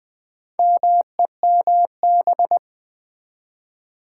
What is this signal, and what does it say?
Morse code, heard as a keyed tone, about 19 wpm, 700 Hz. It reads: MEMB
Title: Build A Word - 4 Letters At A Time - 10-Letter Words - 20wpm